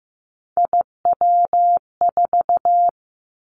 Morse code, heard as a keyed tone, about 15 wpm, 700 Hz. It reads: IW4
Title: Callsign Prefixes for Top 25 DXCC - 15wpm